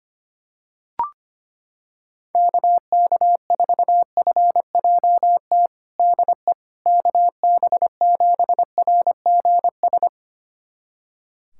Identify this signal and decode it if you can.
Morse code, heard as a keyed tone, about 25 wpm, 700 Hz. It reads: KK4FJT DE KB7RGS
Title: USA Callsign DE Callsign - 6 Characters - 25wpm